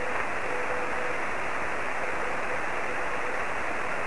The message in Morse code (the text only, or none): none